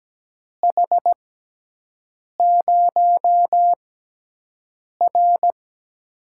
Morse code, heard as a keyed tone, about 17 wpm, spaced as for 6 wpm, 700 Hz. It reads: H0R
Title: USA Call Signs - 4 Characters - 6x Character Spacing - 17wpm